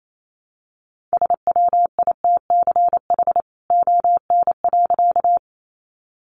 Morse code, heard as a keyed tone, about 28 wpm, 700 Hz. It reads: SWITCH ON.
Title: A Sound Of Thunder 28wpm